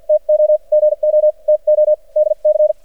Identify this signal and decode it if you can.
Morse code, heard as a keyed tone, about 53 wpm, spaced as for 42 wpm, 610 Hz. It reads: TO GO TO DC
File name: archived / CRSnet